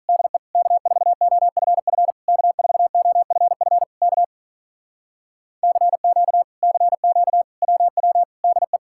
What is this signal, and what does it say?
Morse code, heard as a keyed tone, about 48 wpm, 700 Hz. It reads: DE K4OFF K4OFF K CQ CQ WW DE